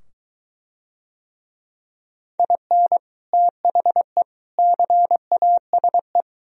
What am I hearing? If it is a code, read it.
Morse code, heard as a keyed tone, about 23 wpm, 700 Hz. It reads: IN THE CASE